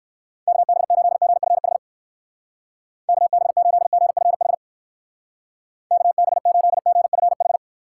Morse code, heard as a keyed tone, about 60 wpm, 700 Hz. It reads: KB8GFH KB8GFH KB8GFH